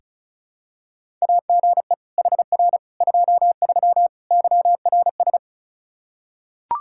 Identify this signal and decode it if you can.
Morse code, heard as a keyed tone, about 34 wpm, 700 Hz. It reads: AGE HR 23 YRS
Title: K1USN SST QSO Practice 35wpm